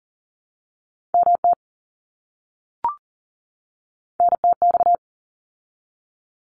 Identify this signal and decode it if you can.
Morse code, heard as a keyed tone, about 40 wpm, 700 Hz. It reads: MT NTX
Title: ARRL Sections 40wpm